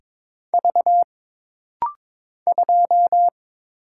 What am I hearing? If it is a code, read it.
Morse code, heard as a keyed tone, about 22 wpm, 700 Hz. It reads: V 2
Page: This rapid-fire timing is geared to push your ICR (Instant Character Recognition) skills.